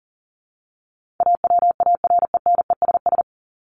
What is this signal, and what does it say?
Morse code, heard as a keyed tone, about 37 wpm, 700 Hz. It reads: AWARENESS
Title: Top 1501-1750 Words 40wpm